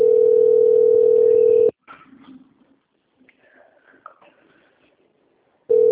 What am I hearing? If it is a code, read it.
Ringback tone (North America), 440 + 480 Hz.
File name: out- - - - - .13987